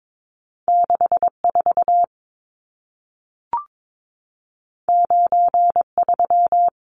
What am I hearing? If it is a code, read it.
Morse code, heard as a keyed tone, about 22 wpm, 700 Hz. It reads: 64 93
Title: Numbers - 2-Digit 22wpm